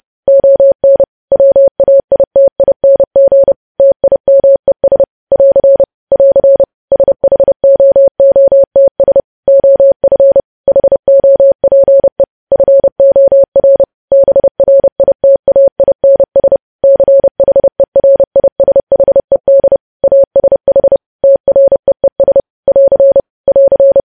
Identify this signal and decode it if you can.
Morse code, heard as a keyed tone, about 30 wpm, 550 Hz. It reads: ON WAITING TIMES <AR> <AR> SHOOTS OF HOPE FOR BRITAINS CHERISHED ASH TREES <AR> <AR>
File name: podcast / News headlines-1x-30x30